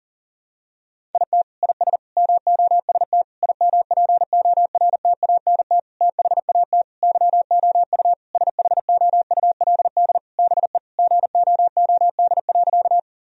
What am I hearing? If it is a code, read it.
Morse code, heard as a keyed tone, about 40 wpm, 700 Hz. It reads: IT IS MOST IMPORTANT THAT YOU SHOULD BE GOOD.
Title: Sentences from Top 1000 Words - Review Part 1 - 40wpm